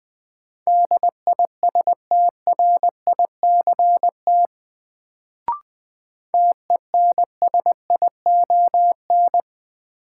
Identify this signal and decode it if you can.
Morse code, heard as a keyed tone, about 20 wpm, 700 Hz. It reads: DISTRICT TENSION
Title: Sentences from Top 1600 Words - New Word Drill Alt - 20wpm